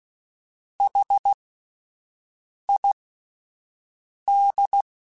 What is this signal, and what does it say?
Morse code, heard as a keyed tone, about 16 wpm, spaced as for 5 wpm, 800 Hz. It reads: HID
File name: Alphabetical / Alphabetical mp3s / Hs @5